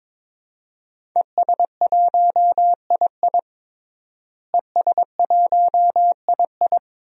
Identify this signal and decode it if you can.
Morse code, heard as a keyed tone, about 21 wpm, 700 Hz. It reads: ES1II ES1II
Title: DXCC-100 Call Signs - 5 Characters - Repeated 3x - 22wpm